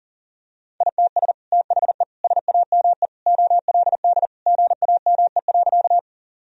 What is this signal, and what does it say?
Morse code, heard as a keyed tone, about 40 wpm, 700 Hz. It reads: ITS THE SAME OLD GAME.
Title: Sentences from Top 1000 Words - Review Part 3 - 40wpm